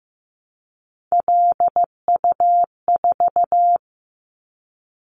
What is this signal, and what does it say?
Morse code, heard as a keyed tone, about 15 wpm, 700 Hz. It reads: LU4
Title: Callsign Prefixes for Top 50 DXCC - 15wpm